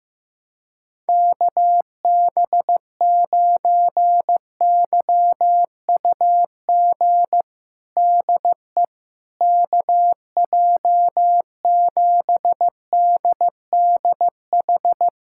Morse code, heard as a keyed tone, about 16 wpm, 700 Hz. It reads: KB9YUG DE KJ7DDH
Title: USA Callsign DE Callsign - 6 Characters - 15wpm